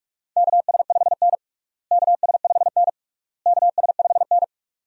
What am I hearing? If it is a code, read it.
Morse code, heard as a keyed tone, about 41 wpm, 700 Hz. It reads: KS5N KS5N KS5N
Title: USA Call Signs - 4 Characters - Repeated 3x - 45wpm